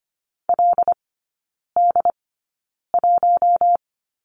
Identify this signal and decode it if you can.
Morse code, heard as a keyed tone, about 25 wpm, spaced as for 9 wpm, 690 Hz. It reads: LD1
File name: Lesson 15e - Number 5 - Three Characters 25wpm